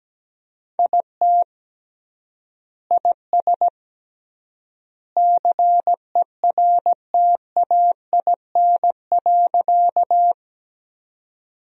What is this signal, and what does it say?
Morse code, heard as a keyed tone, about 17 wpm, 700 Hz. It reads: IT IS CERTAIN.
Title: 3x Word-Spacing - Sentences from Top 500 Words - 17wpm